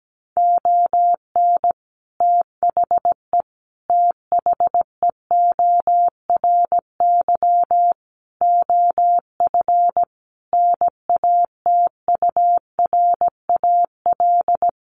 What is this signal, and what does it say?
Morse code, heard as a keyed tone, about 17 wpm, 700 Hz. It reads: ON THE THEORY OF NATURAL
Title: Sentences from Top 1400 Words 17wpm